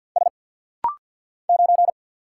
Morse code, heard as a keyed tone, about 50 wpm, 690 Hz. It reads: S 9
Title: Single Letter-Number Alt - Rapid-Fire 50wpm